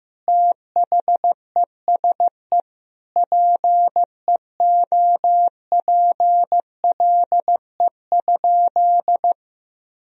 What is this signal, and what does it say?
Morse code, heard as a keyed tone, about 15 wpm, 700 Hz. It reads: THESE PEOPLE?
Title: Sentences from Top 1000 Words - Review Part 1 - 15wpm